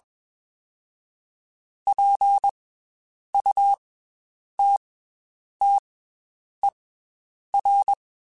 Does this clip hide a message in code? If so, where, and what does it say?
Yes: Morse code from 1.9 to 7.9 s, heard as a keyed tone, about 21 wpm, 800 Hz. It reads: P U T T E R